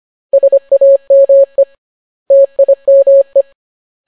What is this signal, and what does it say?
Morse code, heard as a keyed tone, about 25 wpm, 550 Hz. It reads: SAME TIME